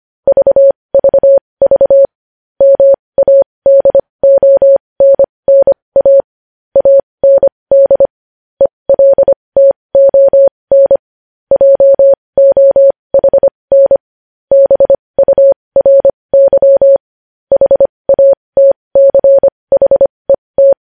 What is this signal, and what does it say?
Morse code, heard as a keyed tone, about 25 wpm, spaced as for 20 wpm, 550 Hz. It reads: VVV MADONNA AND ELTON JOHN BURY HATCHET
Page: news headlines in morse code at 20 wpm